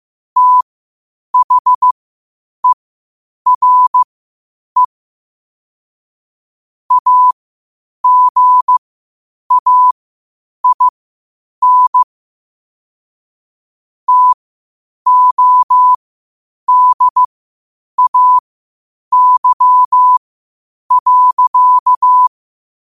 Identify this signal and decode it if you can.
Morse code, heard as a keyed tone, about 15 wpm, spaced as for 8 wpm, 1000 Hz. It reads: THERE AGAIN TODAY.
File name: qotdfiles / 8wpm